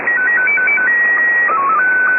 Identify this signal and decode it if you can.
SSTV (slow-scan television), Scottie 1.
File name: items / Httparchive.orgdetailsShortwavePirateRadioCont2 / AmphetamineRadio6930Usb0309z-0344z09-27-15